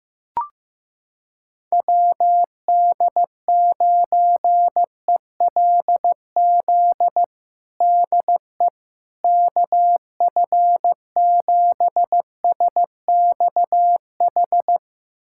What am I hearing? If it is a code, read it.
Morse code, heard as a keyed tone, about 15 wpm, 700 Hz. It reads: WD9ELZ DE KF7SXH
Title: USA Callsign DE Callsign - 6 Characters - 15wpm